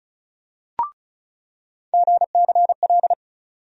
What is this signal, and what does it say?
Morse code, heard as a keyed tone, about 36 wpm, 700 Hz. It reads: GCL
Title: USA Call Sign Postfixes - 3 Letters 35wpm